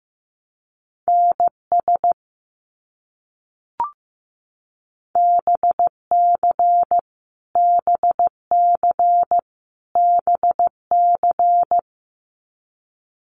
Morse code, heard as a keyed tone, about 15 wpm, 700 Hz. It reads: NS BC BC BC
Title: Canadian Provinces and Territories - Repeated 3x - 15wpm